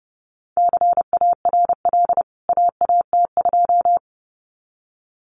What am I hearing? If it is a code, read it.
Morse code, heard as a keyed tone, about 30 wpm, 700 Hz. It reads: CARL AAT2
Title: CWT Exchange 30wpm